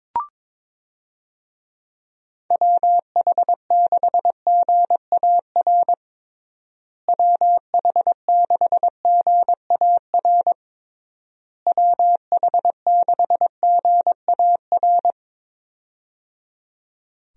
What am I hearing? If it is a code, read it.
Morse code, heard as a keyed tone, about 22 wpm, 700 Hz. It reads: WH6GAR WH6GAR WH6GAR
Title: USA Call Signs - 6 Characters - Repeated 3x - 22wpm